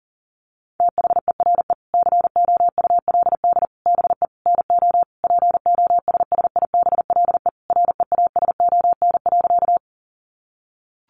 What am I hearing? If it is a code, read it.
Morse code, heard as a keyed tone, about 37 wpm, 700 Hz. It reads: THERE COULD BE NO POSSIBLE REASON.
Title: Sentences from Top 700 Words 40wpm